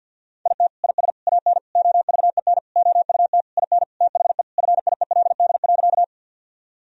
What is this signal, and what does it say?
Morse code, heard as a keyed tone, about 50 wpm, 700 Hz. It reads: IT IS AN OVEN OUT IN THE FIELD.
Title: Sentences from Top 1000 Words - Review Part 4 - 50wpm